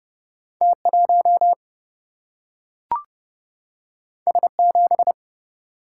Morse code, heard as a keyed tone, about 30 wpm, 700 Hz. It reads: T 1 S 7
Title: Single-Letter Single-Number 30wpm